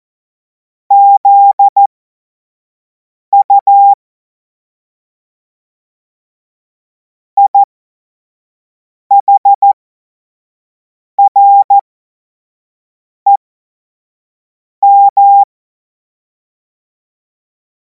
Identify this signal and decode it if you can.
Morse code, heard as a keyed tone, about 14 wpm, spaced as for 5 wpm, 800 Hz. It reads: ZU IHREM
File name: Morsekurs / Witze / Witz 16. Juni 2026 25 F70